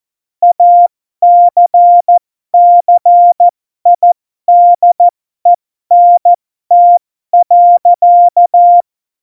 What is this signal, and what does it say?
Morse code, heard as a keyed tone, about 14 wpm, spaced as for 12 wpm, 700 Hz. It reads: ACCIDENT.
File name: Morsekurs / Witze